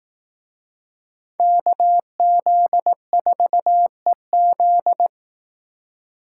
Morse code, heard as a keyed tone, about 18 wpm, 700 Hz. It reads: KZ4EZ
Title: Speed Racing - USA Call Signs 15wpm